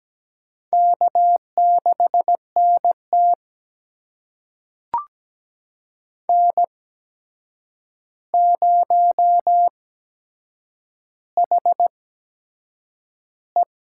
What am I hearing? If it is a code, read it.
Morse code, heard as a keyed tone, about 17 wpm, spaced as for 7 wpm, 700 Hz. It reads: K6NT N0HE
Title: USA Call Signs - 4 Characters - 8x Character Spacing - 17wpm